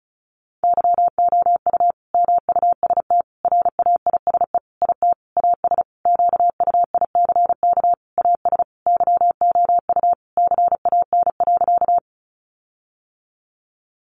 Morse code, heard as a keyed tone, about 34 wpm, 700 Hz. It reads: YOU MUST RAISE IT AS QUICK AS YOU CAN.
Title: Sentences from Top 1400 Words 35wpm